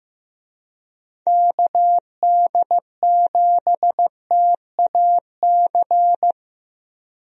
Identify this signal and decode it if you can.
Morse code, heard as a keyed tone, about 15 wpm, 700 Hz. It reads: KD7TAC
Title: USA Call Signs - 6 Characters - 15wpm